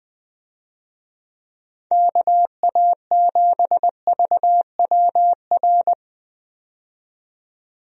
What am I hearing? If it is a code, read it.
Morse code, heard as a keyed tone, about 20 wpm, 700 Hz. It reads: KA7VWR
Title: USA Call Signs - 6 Characters - 20wpm